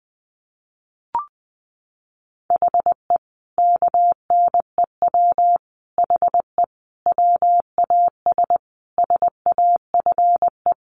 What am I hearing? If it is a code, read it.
Morse code, heard as a keyed tone, about 20 wpm, 700 Hz. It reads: HE KNEW HE WAS SAFE
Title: Sentences from Top 900 Words 20wpm